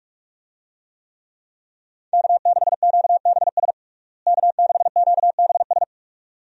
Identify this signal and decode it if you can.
Morse code, heard as a keyed tone, about 46 wpm, 700 Hz. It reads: K6QBS K6QBS
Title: USA Call Signs - 5 Characters - Repeated 2x - 45wpm